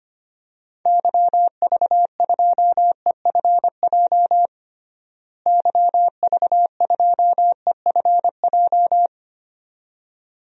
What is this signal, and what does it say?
Morse code, heard as a keyed tone, about 25 wpm, 700 Hz. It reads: YV2EFJ YV2EFJ
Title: DXCC-100 Call Signs - 6 Characters - Repeated 2x - 25wpm